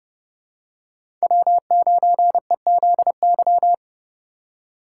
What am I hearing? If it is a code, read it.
Morse code, heard as a keyed tone, about 30 wpm, 700 Hz. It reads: W9EZY
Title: USA Call Signs - 5 Characters - 2x Character Spacing - 30wpm